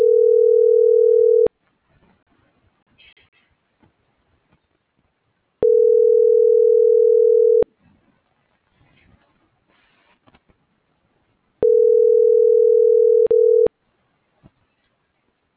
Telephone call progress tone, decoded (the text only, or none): ringback tone